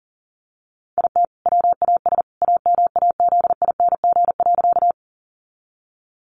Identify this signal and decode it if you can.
Morse code, heard as a keyed tone, about 40 wpm, 700 Hz. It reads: IT WAS AMAZING.
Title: Sentences from Top 1500 Words 40wpm